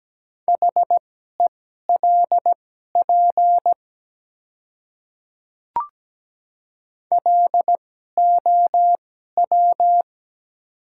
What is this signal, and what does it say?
Morse code, heard as a keyed tone, about 17 wpm, spaced as for 12 wpm, 700 Hz. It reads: HELP LOW
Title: 2x Character Spacing - Top 300 Words - 17wpm